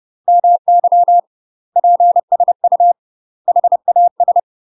Morse code, heard as a keyed tone, about 30 wpm, 700 Hz. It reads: MY PSU HAS
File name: UK MASTER QSOs TRAINING/30WPM